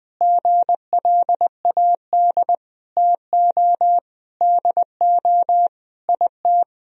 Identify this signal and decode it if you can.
Morse code, heard as a keyed tone, about 20 wpm, 700 Hz. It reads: GLAD TO DO IT
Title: Sentences from Top 1000 Words - Review Part 3 - 20wpm